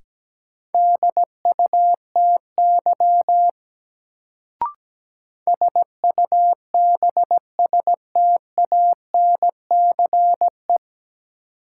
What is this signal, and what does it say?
Morse code, heard as a keyed tone, about 17 wpm, 700 Hz. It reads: DUTY SUBSTANCE
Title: Sentences from Top 1500 Words - New Word Drill Alt - 17wpm